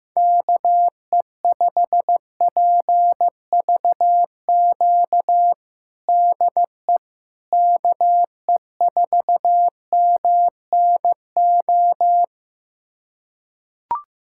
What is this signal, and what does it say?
Morse code, heard as a keyed tone, about 15 wpm, 700 Hz. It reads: KE5PVQ DE KE4MNO
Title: USA Callsign DE Callsign 15wpm